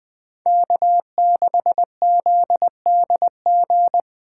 Morse code, heard as a keyed tone, about 20 wpm, 700 Hz. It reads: K6ZDG